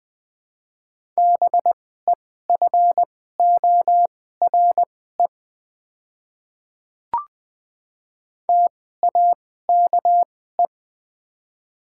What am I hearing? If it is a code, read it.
Morse code, heard as a keyed tone, about 20 wpm, spaced as for 14 wpm, 700 Hz. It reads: BEFORE TAKE
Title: 2x Character Spacing - Top 200 Words - 20wpm